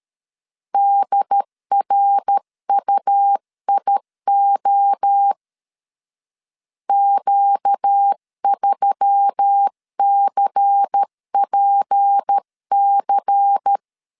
Morse code, heard as a keyed tone, about 13 wpm, 800 Hz. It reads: DRUIO Q3CPC
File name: CW / DET / 1 6 1500C S C 12wpm 3533